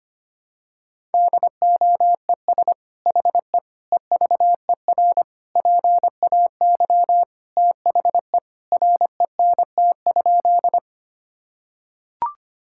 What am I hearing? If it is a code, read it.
Morse code, heard as a keyed tone, about 25 wpm, 700 Hz. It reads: DOES HE EVER PAY THE RENT?
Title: Sentences from Top 1000 Words - Review Part 5 - 25wpm